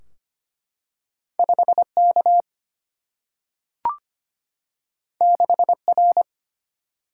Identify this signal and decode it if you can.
Morse code, heard as a keyed tone, about 25 wpm, 700 Hz. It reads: 5K 6R
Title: Single-Letter Single-Number 25wpm